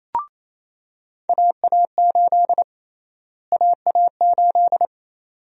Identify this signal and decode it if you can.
Morse code, heard as a keyed tone, about 28 wpm, 700 Hz. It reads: AA8 AA8
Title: USA Call Sign Prefixes - Repeated 2x - 28wpm